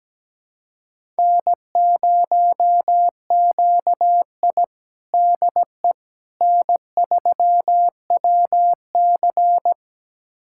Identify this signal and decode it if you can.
Morse code, heard as a keyed tone, about 17 wpm, 700 Hz. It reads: N0QI DE N3WC
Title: USA Callsign DE Callsign - 4 Characters - 17wpm